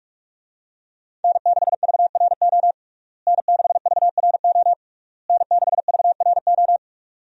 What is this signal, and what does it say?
Morse code, heard as a keyed tone, about 46 wpm, 700 Hz. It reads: N6VRO N6VRO N6VRO
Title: USA Call Signs - 5 Characters - Repeated 3x - 45wpm